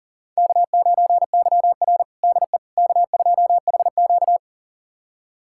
Morse code, heard as a keyed tone, about 40 wpm, 700 Hz. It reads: K9YR DE K2HQ
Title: USA Callsign DE Callsign - 4 Characters - 40wpm